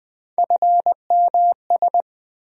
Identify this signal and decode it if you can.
Morse code, heard as a keyed tone, about 20 wpm, 700 Hz. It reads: FMS
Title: USA Call Sign Postfixes - 3 Letters 20wpm